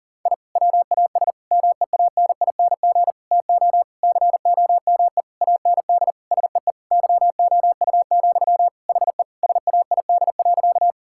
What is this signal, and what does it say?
Morse code, heard as a keyed tone, about 37 wpm, 700 Hz. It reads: I WAS MEANING TO COME AND SEE YOU, HE SAID.
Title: Sentences from Top 1000 Words - Review Part 6 - 40wpm